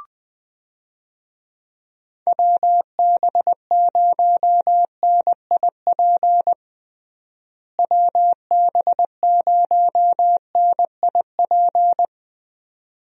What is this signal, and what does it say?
Morse code, heard as a keyed tone, about 20 wpm, 700 Hz. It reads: WB0NIP WB0NIP
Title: USA Call Signs - 6 Characters - Repeated 2x - 20wpm